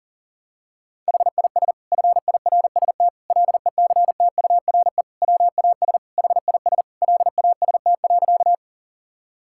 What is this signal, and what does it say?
Morse code, heard as a keyed tone, about 40 wpm, 700 Hz. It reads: HIS FIRST LECTURE WAS HIS LAST.
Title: Sentences from Top 1600 Words 40wpm